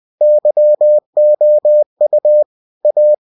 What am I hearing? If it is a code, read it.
Morse code, heard as a keyed tone, about 20 wpm, 600 Hz. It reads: YOU A